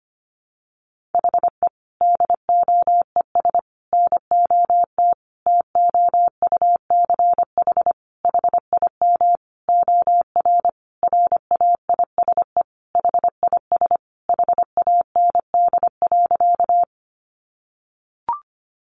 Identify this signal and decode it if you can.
Morse code, heard as a keyed tone, about 25 wpm, 700 Hz. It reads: HE DOES NOT TOUCH HIM OR RAISE HIS HAND.
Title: Sentences from Top 1400 Words 25wpm